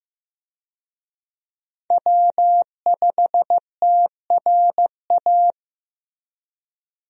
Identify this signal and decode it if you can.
Morse code, heard as a keyed tone, about 15 wpm, 700 Hz. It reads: W5TRA
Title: USA Call Signs - 5 Characters - Repeated 2x - 15wpm